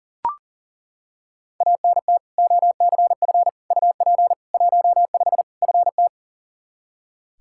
Morse code, heard as a keyed tone, about 40 wpm, 700 Hz. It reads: ANT OCF UP 15 FT
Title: Practice QSO - Exchange 2 - Info - 40wpm